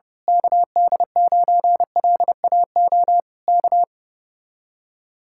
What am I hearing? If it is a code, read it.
Morse code, heard as a keyed tone, about 29 wpm, 700 Hz. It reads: KD9LAO K
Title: Speed-Racing Common QSO Phrases - 25wpm